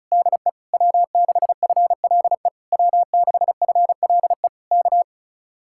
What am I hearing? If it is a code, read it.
Morse code, heard as a keyed tone, about 34 wpm, 700 Hz. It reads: DE W6FLE W6FLE K
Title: Speed-Racing Common QSO Phrases Encore - 30wpm